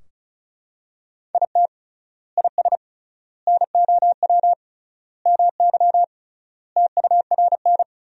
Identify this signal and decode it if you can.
Morse code, heard as a keyed tone, about 34 wpm, 700 Hz. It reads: IT IS NOW MY TURN
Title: 3x Word-Spacing - Sentences from Top 200 Words - 35wpm